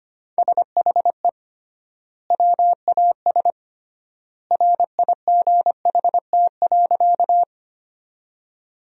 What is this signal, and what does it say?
Morse code, heard as a keyed tone, about 25 wpm, 700 Hz. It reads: SHE WAS RIGHT.
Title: 3x Word-Spacing - Sentences from Top 200 Words - 25wpm